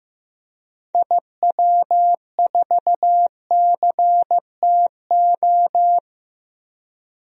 Morse code, heard as a keyed tone, about 15 wpm, 700 Hz. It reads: IW4CTO
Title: DXCC-100 Call Signs - 6 Characters - Repeated 3x - 15wpm